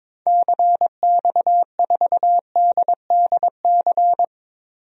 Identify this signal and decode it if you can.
Morse code, heard as a keyed tone, about 22 wpm, 700 Hz. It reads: CX4DDC
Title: DXCC-100 Call Signs - 6 Characters - Repeated 3x - 22wpm